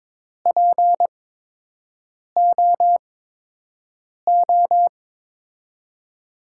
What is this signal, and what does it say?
Morse code, heard as a keyed tone, about 22 wpm, spaced as for 6 wpm, 700 Hz. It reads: POO
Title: USA Call Signs - 5 Characters - 8x Character Spacing - 22wpm